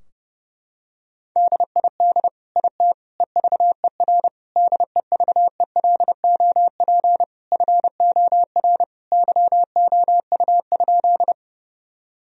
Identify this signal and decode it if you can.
Morse code, heard as a keyed tone, about 32 wpm, 700 Hz. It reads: DID IT EVER DEVELOP FOR YOU?
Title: Sentences from Top 1000 Words - Review Part 2 - 30wpm